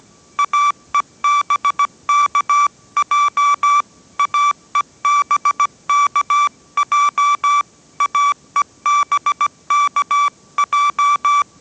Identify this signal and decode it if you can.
Morse code, heard as a keyed tone, about 17 wpm, 1160 Hz. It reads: AEBKJ AEBKJ AEBKJ